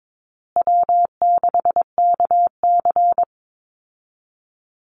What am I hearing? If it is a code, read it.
Morse code, heard as a keyed tone, about 22 wpm, 700 Hz. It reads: W6KC
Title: Call Signs Repeated 3x - 22wpm